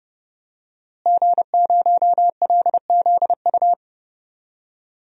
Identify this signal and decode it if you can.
Morse code, heard as a keyed tone, about 31 wpm, 700 Hz. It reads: G0LZU
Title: DXCC-100 Call Signs - 5 Characters - Repeated 3x - 30wpm